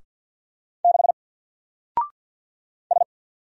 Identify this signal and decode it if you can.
Morse code, heard as a keyed tone, about 48 wpm, 700 Hz. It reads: 6 S
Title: Single Letter-Number Alt 50wpm